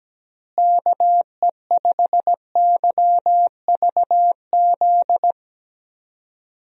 Morse code, heard as a keyed tone, about 17 wpm, 700 Hz. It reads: KE5YVZ
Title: USA Call Signs - 6 Characters - Repeated 3x - 17wpm